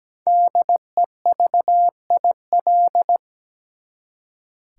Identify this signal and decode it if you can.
Morse code, heard as a keyed tone, about 17 wpm, 700 Hz. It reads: DEVIL
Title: Five Letter Words - 17wpm